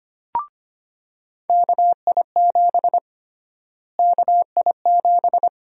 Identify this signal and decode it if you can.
Morse code, heard as a keyed tone, about 25 wpm, 700 Hz. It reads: KI7 KI7
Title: USA Call Sign Prefixes - Repeated 2x - 25wpm